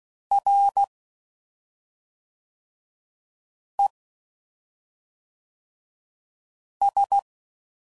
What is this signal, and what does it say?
Morse code, heard as a keyed tone, about 16 wpm, 800 Hz. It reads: R E S
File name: Basic Letter Groups By7Groups / EISH5 / EISH5@5 f15